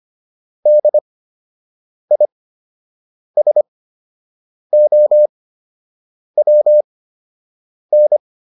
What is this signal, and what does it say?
Morse code, heard as a keyed tone, about 25 wpm, spaced as for 7 wpm, 600 Hz. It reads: DISOWN